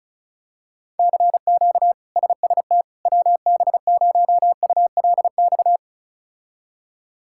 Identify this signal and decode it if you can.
Morse code, heard as a keyed tone, about 36 wpm, 700 Hz. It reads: CQ SST WB0ULX
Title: K1USN SST QSO Practice - 35wpm